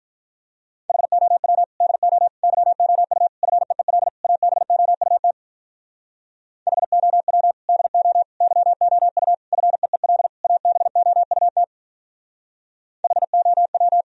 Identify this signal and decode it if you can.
Morse code, heard as a keyed tone, about 48 wpm, 700 Hz. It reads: HOW DO YOU FEEL ABOUT HOW DO YOU FEEL ABOUT HOW